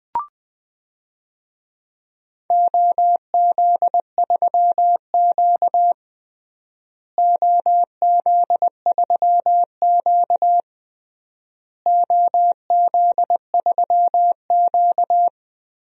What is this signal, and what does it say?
Morse code, heard as a keyed tone, about 20 wpm, 700 Hz. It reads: OZ3Q OZ3Q OZ3Q
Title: DXCC-100 Call Signs - 4 Characters - Repeated 3x - 20wpm